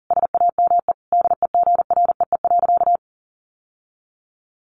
Morse code, heard as a keyed tone, about 40 wpm, 700 Hz. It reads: SAME DEGREE.
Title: Sentences from Top 1500 Words 40wpm